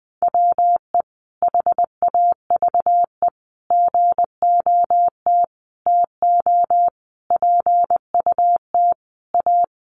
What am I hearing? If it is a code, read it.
Morse code, heard as a keyed tone, about 19 wpm, 700 Hz. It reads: WE HAVE GOT TO PUT A
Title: Sentences from Top 1200 Words 20wpm